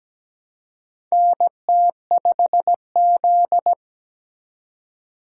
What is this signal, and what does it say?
Morse code, heard as a keyed tone, about 17 wpm, 700 Hz. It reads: NT5Z
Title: USA Call Signs - 4 Characters - 4x Character Spacing - 17wpm